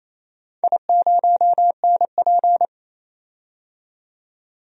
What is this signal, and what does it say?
Morse code, heard as a keyed tone, about 28 wpm, 700 Hz. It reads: I0NP
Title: DXCC-100 Call Signs - 4 Characters - 28wpm